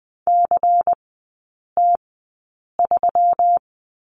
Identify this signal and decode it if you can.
Morse code, heard as a keyed tone, about 20 wpm, spaced as for 8 wpm, 690 Hz. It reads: CT3
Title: 191 Lesson 31e - Letter v - Three Characters 20wpm